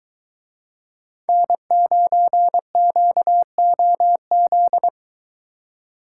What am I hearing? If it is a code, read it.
Morse code, heard as a keyed tone, about 23 wpm, 700 Hz. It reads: N9QOZ